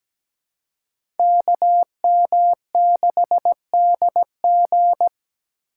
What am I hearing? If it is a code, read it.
Morse code, heard as a keyed tone, about 17 wpm, 700 Hz. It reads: KM6DG
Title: USA Call Signs - 4x Character Spacing - 17wpm